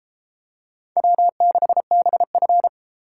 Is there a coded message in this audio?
Yes: Morse code from 1.0 to 2.7 s, heard as a keyed tone, about 30 wpm, 700 Hz. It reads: W6BF